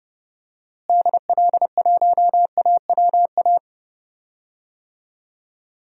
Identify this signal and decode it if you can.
Morse code, heard as a keyed tone, about 30 wpm, 700 Hz. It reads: DL1AWA
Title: DXCC-100 Call Signs - 6 Characters - Repeated 3x - 30wpm